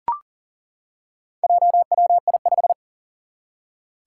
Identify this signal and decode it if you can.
Morse code, heard as a keyed tone, about 40 wpm, 700 Hz. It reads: JWI5
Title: Four Challenging Letter-Number Combinations 40wpm